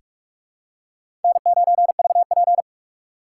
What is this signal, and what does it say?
Morse code, heard as a keyed tone, about 41 wpm, 700 Hz. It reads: N9VP
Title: USA Call Signs - 6x Character Spacing - 45wpm